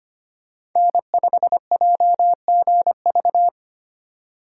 Morse code, heard as a keyed tone, about 25 wpm, 700 Hz. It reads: N5JGV
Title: USA Call Signs - 3x Character Spacing - 25wpm